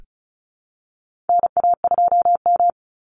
Morse code, heard as a keyed tone, about 34 wpm, 700 Hz. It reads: NA2M
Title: Call Signs Repeated 2x - 35wpm